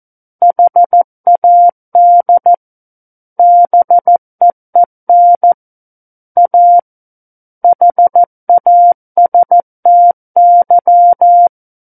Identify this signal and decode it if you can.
Morse code, heard as a keyed tone, about 14 wpm, 700 Hz. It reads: HAD BEEN A HASTY
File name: Book One / 05-Chapter 5 - 14 WPM